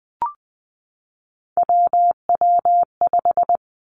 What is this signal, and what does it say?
Morse code, heard as a keyed tone, about 20 wpm, 700 Hz. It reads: WW5
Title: Three Challenging Letter-Number Combinations 20wpm